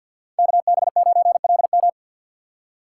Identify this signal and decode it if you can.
Morse code, heard as a keyed tone, about 52 wpm, 700 Hz. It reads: K B 9 L M
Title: USA Call Signs - 5 Characters - 6x Character Spacing - 50wpm